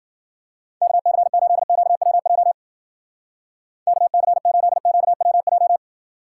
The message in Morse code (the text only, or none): KC8QWJ KC8QWJ